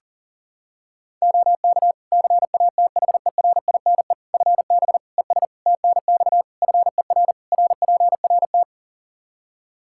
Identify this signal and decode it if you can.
Morse code, heard as a keyed tone, about 40 wpm, 700 Hz. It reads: OK CATHERINE FB ES TNX FER RPRT
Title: Practice QSO - Exchange 2 - Info - 40wpm